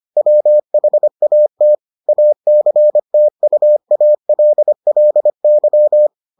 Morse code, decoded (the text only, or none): WHAT ACTUALLY